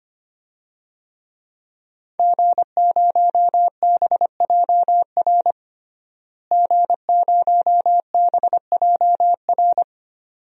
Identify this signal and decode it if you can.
Morse code, heard as a keyed tone, about 25 wpm, 700 Hz. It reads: G0BJR G0BJR
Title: DXCC-100 Call Signs - 5 Characters - Repeated 2x - 25wpm